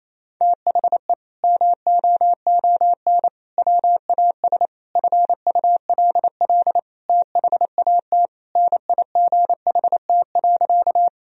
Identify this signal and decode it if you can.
Morse code, heard as a keyed tone, about 28 wpm, 700 Hz. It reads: THE MOON WAS FULL THAT NIGHT.
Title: Sentences from Top 1000 Words - Review Part 3 - 28wpm